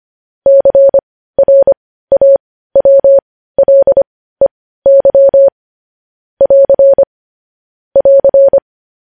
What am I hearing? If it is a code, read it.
Morse code, heard as a keyed tone, about 25 wpm, spaced as for 15 wpm, 550 Hz. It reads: CRAWLEY <AR> <AR>
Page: news-headlines-in-morse-code-at-15-wpm